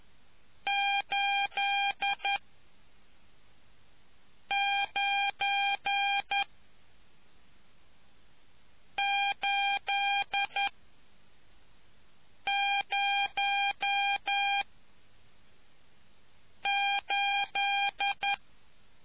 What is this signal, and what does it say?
Morse code, heard as a keyed tone, about 11 wpm, spaced as for 4 wpm, 800 Hz. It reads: 89808